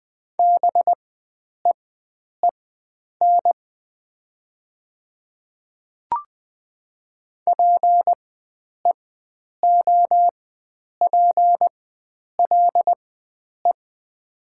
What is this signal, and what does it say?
Morse code, heard as a keyed tone, about 20 wpm, spaced as for 9 wpm, 700 Hz. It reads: BEEN PEOPLE
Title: 4x Character Spacing - Top 100 Words - 20wpm